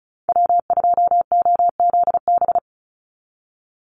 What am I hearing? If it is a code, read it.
Morse code, heard as a keyed tone, about 34 wpm, 700 Hz. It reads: W2OZB
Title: Call Signs Repeated 3x - 35wpm